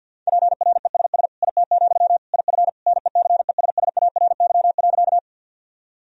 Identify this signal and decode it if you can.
Morse code, heard as a keyed tone, about 49 wpm, 700 Hz. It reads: PRESS IT, IF NECESSARY.
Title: Sentences from Top 1000 Words - Review Part 1 - 50wpm